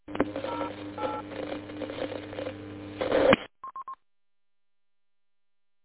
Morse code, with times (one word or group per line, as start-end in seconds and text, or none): none